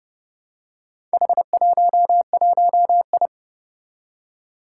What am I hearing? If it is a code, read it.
Morse code, heard as a keyed tone, about 30 wpm, 700 Hz. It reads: H11I
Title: Four Challenging Letter-Number Combinations 30wpm